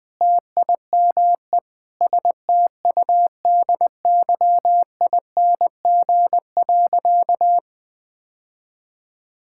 Morse code, heard as a keyed tone, about 20 wpm, 700 Hz. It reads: TIME STUDYING.
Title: Sentences with 6-Word Phrases - 20wpm